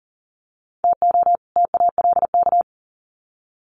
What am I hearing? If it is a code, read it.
Morse code, heard as a keyed tone, about 40 wpm, 700 Hz. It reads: TO TALK
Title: Sets of 2 Words 40wpm